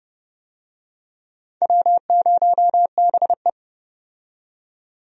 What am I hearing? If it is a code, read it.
Morse code, heard as a keyed tone, about 32 wpm, 700 Hz. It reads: W0BE
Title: USA Call Signs - 4 Characters - 30wpm